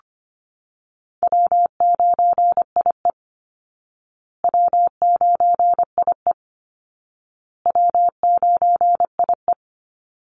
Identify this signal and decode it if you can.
Morse code, heard as a keyed tone, about 25 wpm, 700 Hz. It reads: W9IE W9IE W9IE
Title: Easy Call Signs Repeated 3x - 25wpm